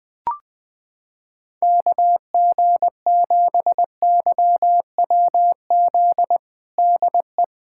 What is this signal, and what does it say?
Morse code, heard as a keyed tone, about 20 wpm, 700 Hz. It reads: KG7YWZ DE
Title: USA Callsign DE Callsign - 6 Characters - 20wpm